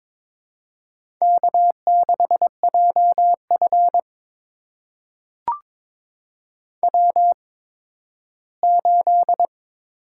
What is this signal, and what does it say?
Morse code, heard as a keyed tone, about 22 wpm, spaced as for 13 wpm, 700 Hz. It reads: K6JF W8
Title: USA Call Signs - 4 Characters - 8x Character Spacing - 22wpm